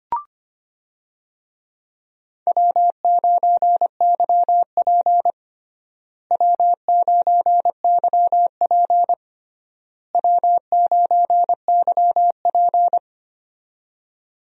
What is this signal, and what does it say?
Morse code, heard as a keyed tone, about 25 wpm, 700 Hz. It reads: W9YP W9YP W9YP
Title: USA Call Signs - 4 Characters - Repeated 3x - 25wpm